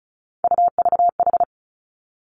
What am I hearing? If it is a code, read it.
Morse code, heard as a keyed tone, about 36 wpm, 700 Hz. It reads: UVH
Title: Three Challenging Letter-Number Combinations 35wpm